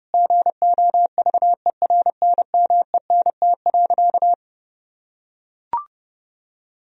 Morse code, heard as a keyed tone, about 30 wpm, 700 Hz. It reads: GOVERNMENT.
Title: Sentences from Top 1000 Words - Review Part 3 - 30wpm